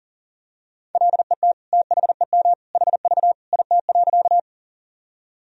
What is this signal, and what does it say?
Morse code, heard as a keyed tone, about 40 wpm, 700 Hz. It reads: LET THEM HV IT.
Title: Sentences from Top 300 Words and Abbreviations 40wpm